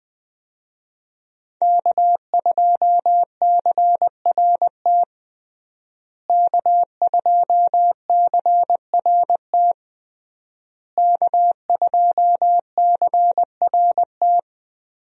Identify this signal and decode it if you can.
Morse code, heard as a keyed tone, about 21 wpm, 700 Hz. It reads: K2CRT K2CRT K2CRT
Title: USA Call Signs - 5 Characters - Repeated 3x - 20wpm